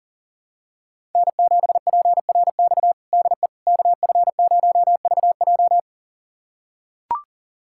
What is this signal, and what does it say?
Morse code, heard as a keyed tone, about 40 wpm, 700 Hz. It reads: N7PRX DE KF0VJ
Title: USA Callsign DE Callsign - 5 Characters - 40wpm